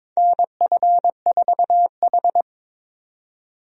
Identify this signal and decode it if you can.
Morse code, heard as a keyed tone, about 22 wpm, 700 Hz. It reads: NF4H
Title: USA Call Signs - 4 Characters - 22wpm